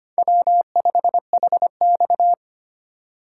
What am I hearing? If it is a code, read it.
Morse code, heard as a keyed tone, about 25 wpm, 700 Hz. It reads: W5HX
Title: USA Call Signs - 4 Characters - 2x Character Spacing - 25wpm